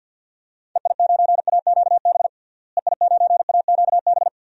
Morse code, heard as a keyed tone, about 46 wpm, 700 Hz. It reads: EI9AQB EI9AQB
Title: DXCC-100 Call Signs - 6 Characters - Repeated 2x - 50wpm